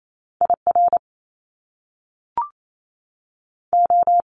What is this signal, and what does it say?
Morse code, heard as a keyed tone, about 28 wpm, 700 Hz. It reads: I R O
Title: Top Two Letter Combinations in Words 28wpm